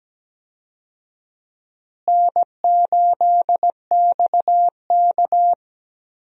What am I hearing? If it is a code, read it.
Morse code, heard as a keyed tone, about 17 wpm, 700 Hz. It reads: N8XK
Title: USA Call Signs - 4 Characters - Repeated 3x - 17wpm